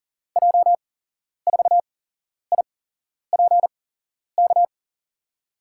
Morse code, heard as a keyed tone, about 39 wpm, spaced as for 11 wpm, 700 Hz. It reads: J4IPK
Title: USA Call Signs - 6 Characters - 8x Character Spacing - 40wpm